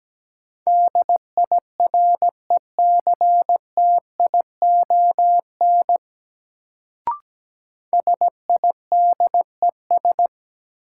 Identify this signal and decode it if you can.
Morse code, heard as a keyed tone, about 17 wpm, 700 Hz. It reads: DIRECTION SIDES
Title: Sets of 4 Words - New Word Drill - 17wpm